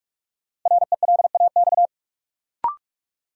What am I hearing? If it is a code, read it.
Morse code, heard as a keyed tone, about 46 wpm, 700 Hz. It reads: RELAX
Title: Sentences from Top 1400 Words - New Word Drill Alt - 45wpm